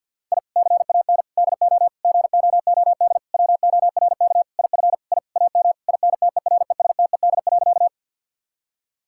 Morse code, heard as a keyed tone, about 50 wpm, 700 Hz. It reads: I CAN DO GOOD WORK IF I AM INTERESTED.
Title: Sentences from Top 1000 Words - Review Part 6 - 50wpm